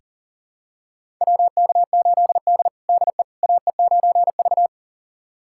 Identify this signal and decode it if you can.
Morse code, heard as a keyed tone, about 40 wpm, 700 Hz. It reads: WK8D DE AE9V
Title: USA Callsign DE Callsign - 4 Characters - 40wpm